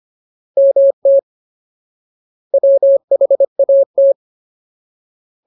Morse code, heard as a keyed tone, about 25 wpm, 550 Hz. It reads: MT WHAT
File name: Ultimate Word List / 25WPM Ultimate